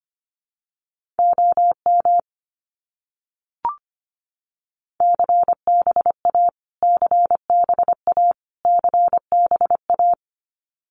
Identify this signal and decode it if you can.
Morse code, heard as a keyed tone, about 25 wpm, 700 Hz. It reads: OM CBA CBA CBA
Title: Abbreviations Repeated 3x - 25wpm